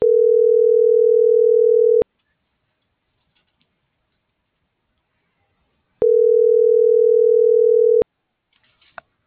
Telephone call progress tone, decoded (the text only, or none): ringback tone